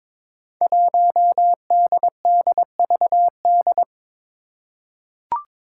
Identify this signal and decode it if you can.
Morse code, heard as a keyed tone, about 22 wpm, 700 Hz. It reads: 1DDVD
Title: Five Challenging Letter-Number Combinations 22wpm